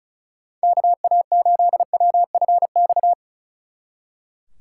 Morse code, heard as a keyed tone, about 48 wpm, spaced as for 42 wpm, 700 Hz. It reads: KA8WFX
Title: Speed Racing - USA 2x3 Call Signs 35wpm